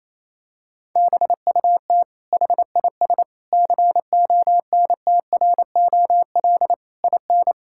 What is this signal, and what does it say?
Morse code, heard as a keyed tone, about 28 wpm, 700 Hz. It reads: BUT HIS CONTROL IN
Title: Sentences from Top 1000 Words - Review Part 6 - 28wpm